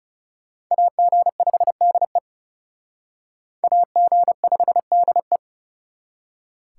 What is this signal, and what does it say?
Morse code, heard as a keyed tone, about 38 wpm, 700 Hz. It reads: AG5DE AG5DE